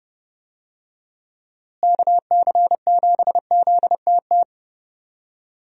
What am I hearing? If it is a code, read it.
Morse code, heard as a keyed tone, about 30 wpm, 700 Hz. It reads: KC7ZTT
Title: USA Call Signs - 6 Characters - Repeated 3x - 30wpm